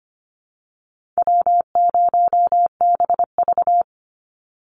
Morse code, heard as a keyed tone, about 25 wpm, 700 Hz. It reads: W0BV
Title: Easy Call Signs - 25wpm